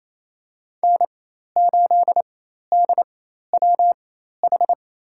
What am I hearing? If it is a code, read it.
Morse code, heard as a keyed tone, about 28 wpm, spaced as for 13 wpm, 700 Hz. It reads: N8DWH
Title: USA Call Signs - 4x Character Spacing - 28wpm